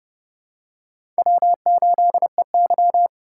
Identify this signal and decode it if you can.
Morse code, heard as a keyed tone, about 30 wpm, 700 Hz. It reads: W8EY
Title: USA Call Signs - 4 Characters - 3x Character Spacing - 30wpm